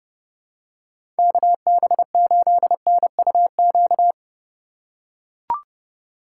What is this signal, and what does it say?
Morse code, heard as a keyed tone, about 30 wpm, 700 Hz. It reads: KB8NUQ
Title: USA Call Signs - 6 Characters - 2x Character Spacing - 30wpm